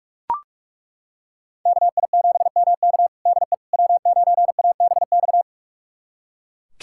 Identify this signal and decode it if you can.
Morse code, heard as a keyed tone, about 42 wpm, 700 Hz. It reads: KI7MK DE W9ABX
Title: USA Callsign DE Callsign - 5 Characters - 45wpm